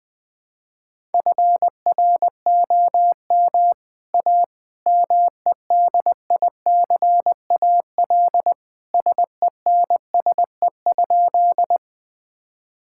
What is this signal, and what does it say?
Morse code, heard as a keyed tone, about 21 wpm, 700 Hz. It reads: FROM A MEDICAL SENSE?
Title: Sentences from Top 1000 Words - Review Part 5 - 20wpm